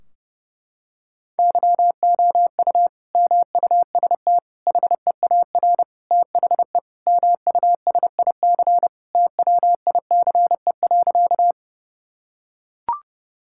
Morse code, heard as a keyed tone, about 30 wpm, 700 Hz. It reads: YOU MUST HEAR THE MUSIC TWICE.
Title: Sentences from Top 1600 Words 30wpm